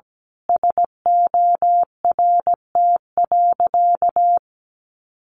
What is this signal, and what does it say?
Morse code, heard as a keyed tone, about 17 wpm, 700 Hz. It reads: SORT.
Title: Sentences from Top 1500 Words 17wpm